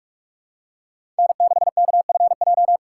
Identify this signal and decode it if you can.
Morse code, heard as a keyed tone, about 42 wpm, 700 Hz. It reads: N6KFJ
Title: USA Call Signs - 5 Characters - 4x Character Spacing - 45wpm